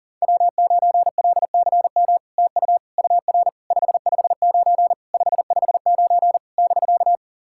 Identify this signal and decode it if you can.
Morse code, heard as a keyed tone, about 43 wpm, 690 Hz. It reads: W9LCM TU UR 559 559 <BK>
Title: SOTA QSO Practice 40wpm